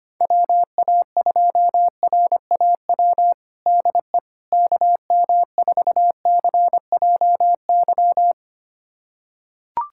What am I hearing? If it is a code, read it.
Morse code, heard as a keyed tone, about 25 wpm, 700 Hz. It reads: WA2RAW DE KM4CJY
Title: USA Callsign DE Callsign - 6 Characters - 25wpm